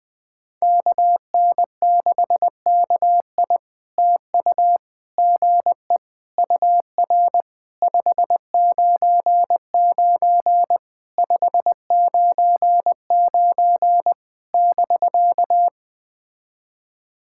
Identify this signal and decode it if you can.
Morse code, heard as a keyed tone, about 20 wpm, 700 Hz. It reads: KN6KI TU GE UR 599 599 <BK>
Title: POTA QSO Practice - 20wpm